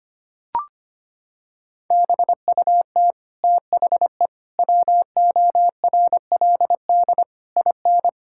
Morse code, heard as a keyed tone, about 25 wpm, 700 Hz. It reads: BUT THE WORLD IN
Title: Sentences from Top 1000 Words - Review Part 6 - 25wpm